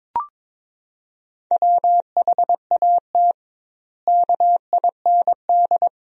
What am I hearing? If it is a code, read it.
Morse code, heard as a keyed tone, about 22 wpm, 700 Hz. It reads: WHAT KIND
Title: 2x Word-Spacing - Sentences from Top 400 Words - 22wpm